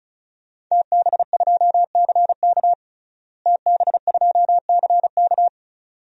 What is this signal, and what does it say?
Morse code, heard as a keyed tone, about 36 wpm, 700 Hz. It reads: TB2CK TB2CK
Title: DXCC-100 Call Signs - 5 Characters - Repeated 2x - 35wpm